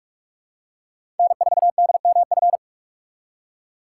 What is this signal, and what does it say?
Morse code, heard as a keyed tone, about 45 wpm, 700 Hz. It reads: N4DMF
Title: USA Call Signs - 5 Characters - 8x Character Spacing - 45wpm